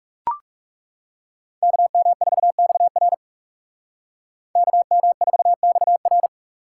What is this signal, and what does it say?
Morse code, heard as a keyed tone, about 37 wpm, 700 Hz. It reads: KM4XR KM4XR
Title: Speed Racing - USA 2x2 Call Signs 30wpm